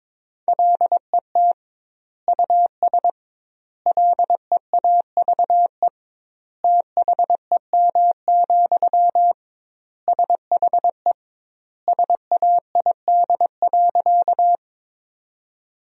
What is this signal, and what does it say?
Morse code, heard as a keyed tone, about 22 wpm, 700 Hz. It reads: LET US LEAVE THEM, SHE SAID.
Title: 2x Word-Spacing - Sentences from Top 400 Words - 22wpm